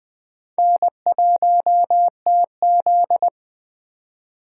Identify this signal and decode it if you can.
Morse code, heard as a keyed tone, about 22 wpm, 700 Hz. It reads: N1TZ